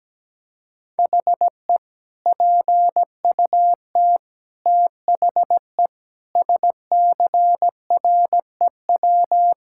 Morse code, heard as a keyed tone, about 17 wpm, 700 Hz. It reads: HE PUT THE SCREW
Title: Sentences from Top 1600 Words 17wpm